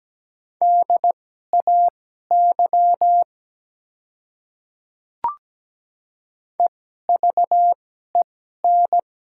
Morse code, heard as a keyed tone, about 17 wpm, spaced as for 12 wpm, 700 Hz. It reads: DAY EVEN
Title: 2x Character Spacing - Top 200 Words - 17wpm